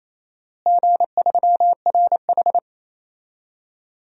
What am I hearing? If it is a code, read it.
Morse code, heard as a keyed tone, about 28 wpm, 700 Hz. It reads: G3RH
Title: DXCC-100 Call Signs - 4 Characters - Repeated 3x - 28wpm